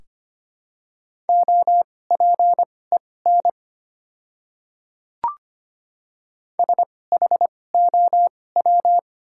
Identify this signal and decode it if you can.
Morse code, heard as a keyed tone, about 25 wpm, spaced as for 18 wpm, 700 Hz. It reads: OPEN SHOW
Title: 2x Character Spacing - Top 300 Words - 25wpm